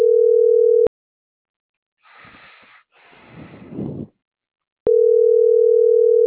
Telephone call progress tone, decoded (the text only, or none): ringback tone